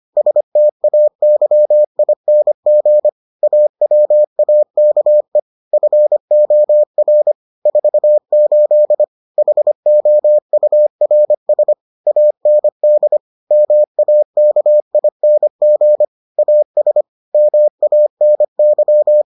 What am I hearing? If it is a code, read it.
Morse code, heard as a keyed tone, about 25 wpm, 600 Hz. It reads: STAYING AWAKE FOR 48 HOURS AND MAKING AS MANY